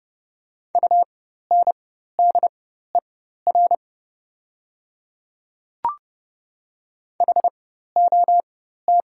Morse code, heard as a keyed tone, about 30 wpm, spaced as for 14 wpm, 690 Hz. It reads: UNDER HOT
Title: 4x Character Spacing - Top 500 Words - 30wpm